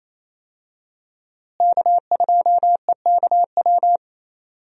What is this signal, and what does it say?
Morse code, heard as a keyed tone, about 28 wpm, 700 Hz. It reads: K2EKW
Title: USA Call Signs - 5 Characters - Repeated 3x - 28wpm